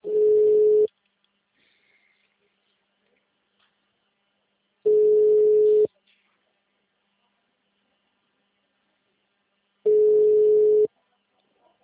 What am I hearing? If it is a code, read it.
Ringback tone (Europe, 425 Hz).
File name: out- -2001- - - .8282